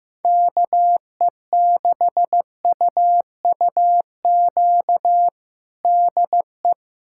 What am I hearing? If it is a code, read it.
Morse code, heard as a keyed tone, about 16 wpm, 700 Hz. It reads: KE6UUQ DE
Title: USA Callsign DE Callsign 15wpm